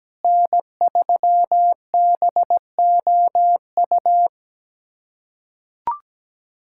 Morse code, heard as a keyed tone, about 17 wpm, 700 Hz. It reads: N3BOU
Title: USA Callsign DE Callsign - 5 Characters - 17wpm